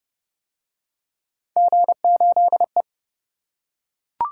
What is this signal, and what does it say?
Morse code, heard as a keyed tone, about 27 wpm, 700 Hz. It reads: G8E